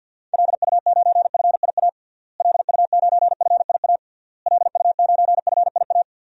Morse code, heard as a keyed tone, about 48 wpm, 700 Hz. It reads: LU9FIA LU9FIA LU9FIA
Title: DXCC-100 Call Signs - 6 Characters - Repeated 3x - 50wpm